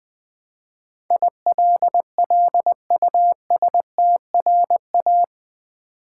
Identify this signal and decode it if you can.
Morse code, heard as a keyed tone, about 20 wpm, 700 Hz. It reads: ILLUSTRA
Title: Build A Word - 4 Letters At A Time - 12-Letter Words - 20wpm